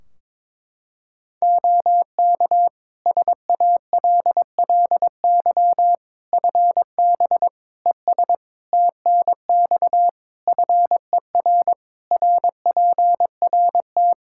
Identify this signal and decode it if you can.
Morse code, heard as a keyed tone, about 22 wpm, 700 Hz. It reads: OK SALLY FB ES TNX FER RPRT
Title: Practice QSO - Exchange 2 - Info - 22wpm